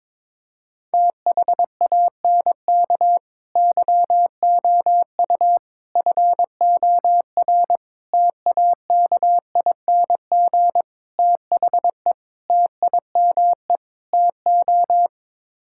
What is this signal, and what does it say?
Morse code, heard as a keyed tone, about 21 wpm, 700 Hz. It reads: THANK YOU FOR TAKING THE TIME TO
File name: practice / 22wpm / Sets of 7 Words - 22wpm